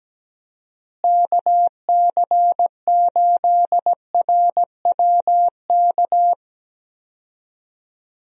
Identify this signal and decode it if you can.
Morse code, heard as a keyed tone, about 17 wpm, 700 Hz. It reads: KC8RWK
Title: USA Call Signs - 6 Characters - Repeated 2x - 17wpm